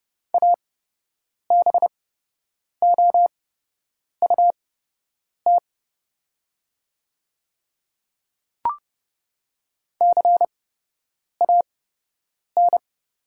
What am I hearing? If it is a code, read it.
Morse code, heard as a keyed tone, about 30 wpm, spaced as for 8 wpm, 700 Hz. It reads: ABOUT CAN
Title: 8x Character Spacing - Top 100 Words - 30wpm